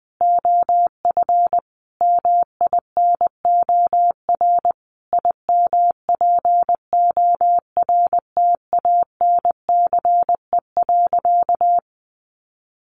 Morse code, heard as a keyed tone, about 20 wpm, 700 Hz. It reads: OF MINOR IMPORTANCE.
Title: Sentences from Top 1500 Words 20wpm